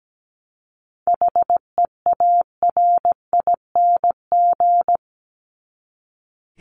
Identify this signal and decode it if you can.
Morse code, heard as a keyed tone, about 17 wpm, 700 Hz. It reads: HEARING
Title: ING Suffix - Top 100 Words with Base Words 17wpm